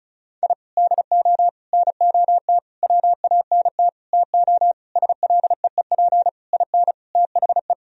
Morse code, heard as a keyed tone, about 36 wpm, 700 Hz. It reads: I DO NOT WANT TO SLEEP IN THE
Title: Sentences from Top 1000 Words - Review Part 2 - 35wpm